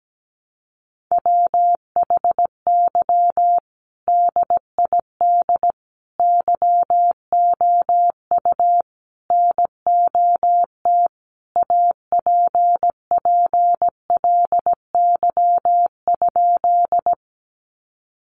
Morse code, heard as a keyed tone, about 17 wpm, 700 Hz. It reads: WHY DID YOU NOT APPLY?
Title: Sentences from Top 1200 Words 17wpm